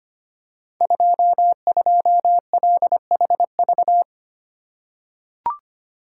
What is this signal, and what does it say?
Morse code, heard as a keyed tone, about 25 wpm, 700 Hz. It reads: 22LHV
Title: Code Groups - 5 Letter:Numbers - 25wpm